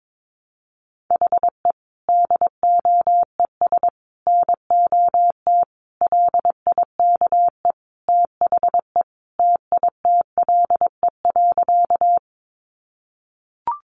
Morse code, heard as a keyed tone, about 22 wpm, 700 Hz. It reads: HE DOES NOT LIKE THE TITLE.
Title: Sentences from Top 1200 Words 22wpm